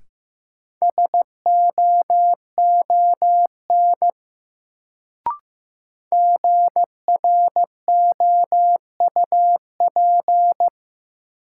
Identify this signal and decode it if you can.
Morse code, heard as a keyed tone, about 15 wpm, 700 Hz. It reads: SOON GROUP
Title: Sets of 3 Words - New Word Drill - 15wpm